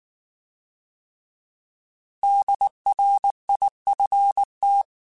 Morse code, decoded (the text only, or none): DRIFT